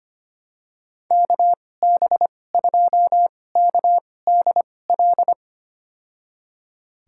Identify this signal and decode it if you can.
Morse code, heard as a keyed tone, about 25 wpm, spaced as for 18 wpm, 700 Hz. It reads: KB2KDL
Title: USA Call Signs - 6 Characters - 2x Character Spacing - 25wpm